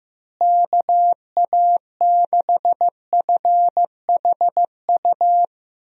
Morse code, heard as a keyed tone, about 15 wpm, 700 Hz. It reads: KA6FHU
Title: USA Call Signs - 6 Characters - Repeated 3x - 15wpm